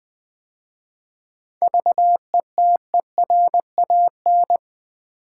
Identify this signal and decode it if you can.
Morse code, heard as a keyed tone, about 20 wpm, 700 Hz. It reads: VETERAN
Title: 7 Letter Words - First 3 Letters Given - 20wpm